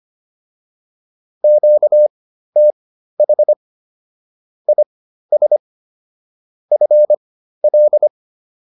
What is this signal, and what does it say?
Morse code, heard as a keyed tone, about 25 wpm, spaced as for 13 wpm, 600 Hz. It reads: QTH IS FL